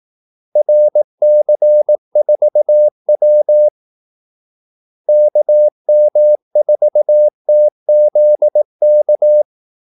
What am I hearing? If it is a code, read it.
Morse code, heard as a keyed tone, about 18 wpm, 600 Hz. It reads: RC4W KM4TZK
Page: Kaldesignaler 18 wpm